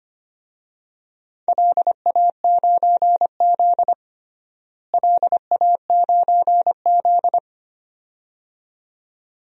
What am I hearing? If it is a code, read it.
Morse code, heard as a keyed tone, about 25 wpm, 700 Hz. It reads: LA9Z LA9Z
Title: DXCC-100 Call Signs - 4 Characters - Repeated 2x - 25wpm